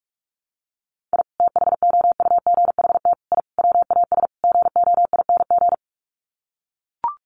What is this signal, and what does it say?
Morse code, heard as a keyed tone, about 44 wpm, 700 Hz. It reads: I THOUGHT I WAS GOING
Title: Sets of 5 Words Encore2 45wpm